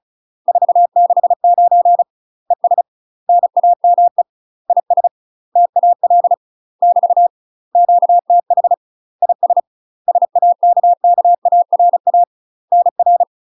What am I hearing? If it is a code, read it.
Morse code, heard as a keyed tone, about 34 wpm, 700 Hz. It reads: 469 ES NAME IS TAL <BT> QTH IS SAKKARA NR
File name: UK MASTER QSOs TRAINING / 35WPM / UK MASTER QSOs 0025